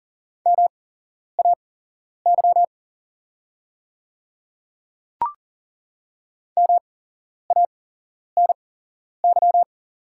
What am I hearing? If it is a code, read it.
Morse code, heard as a keyed tone, about 40 wpm, spaced as for 11 wpm, 700 Hz. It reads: MAY MANY
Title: 8x Character Spacing - Top 100 Words - 40wpm